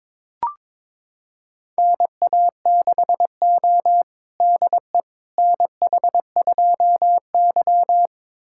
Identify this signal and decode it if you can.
Morse code, heard as a keyed tone, about 21 wpm, 700 Hz. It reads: NA6O DE NH2Y
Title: USA Callsign DE Callsign - 4 Characters - 22wpm